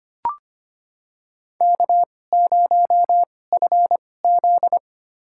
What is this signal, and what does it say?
Morse code, heard as a keyed tone, about 25 wpm, spaced as for 18 wpm, 700 Hz. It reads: K0FZ
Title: USA Call Signs - 4 Characters - 2x Character Spacing - 25wpm